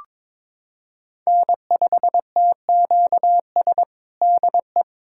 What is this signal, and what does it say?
Morse code, heard as a keyed tone, about 21 wpm, 700 Hz. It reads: N5TQS DE
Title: USA Callsign DE Callsign - 5 Characters - 22wpm